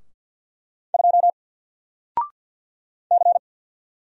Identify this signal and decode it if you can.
Morse code, heard as a keyed tone, about 49 wpm, 700 Hz. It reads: 2 C
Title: Single Character (Letter:Number:Punctuation:Prosign) 50wpm